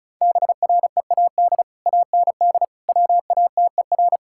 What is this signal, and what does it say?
Morse code, heard as a keyed tone, about 34 wpm, 700 Hz. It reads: BREAD AND WATER
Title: Sentences from Top 1600 Words 35wpm